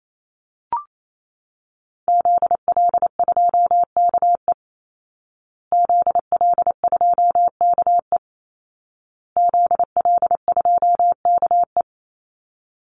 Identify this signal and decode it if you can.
Morse code, heard as a keyed tone, about 28 wpm, 700 Hz. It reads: ZL2KE ZL2KE ZL2KE
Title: Call Signs Repeated 2x - 28wpm